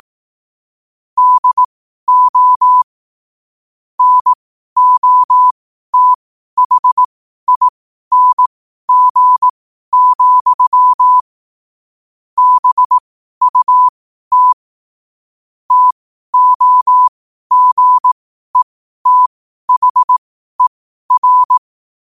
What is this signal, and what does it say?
Morse code, heard as a keyed tone, about 18 wpm, spaced as for 12 wpm, 1000 Hz. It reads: DO NOTHING, BUT TOGETHER
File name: qotdfiles / 12wpm